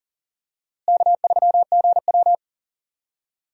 Morse code, heard as a keyed tone, about 40 wpm, 700 Hz. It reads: K3GW
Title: USA Call Signs - 4 Characters - Repeated 3x - 40wpm